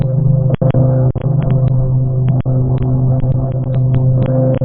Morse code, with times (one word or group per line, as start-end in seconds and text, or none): none